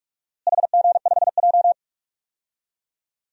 Morse code, heard as a keyed tone, about 43 wpm, 700 Hz. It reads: HG5J
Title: DXCC-100 Call Signs - 4 Characters - Repeated 3x - 45wpm